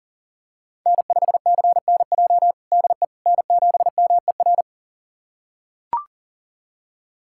Morse code, heard as a keyed tone, about 40 wpm, 700 Hz. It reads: N5CNJ DE N7MER
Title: USA Callsign DE Callsign - 5 Characters - 40wpm